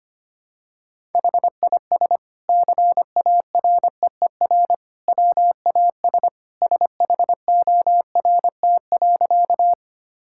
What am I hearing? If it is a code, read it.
Morse code, heard as a keyed tone, about 25 wpm, 700 Hz. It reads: HIS CAREER WAS SHORT.
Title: Sentences from Top 1000 Words - Review Part 6 - 25wpm